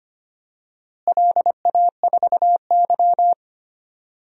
Morse code, heard as a keyed tone, about 25 wpm, 700 Hz. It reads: LA4Y
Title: DXCC-100 Call Signs - 4 Characters - Repeated 3x - 25wpm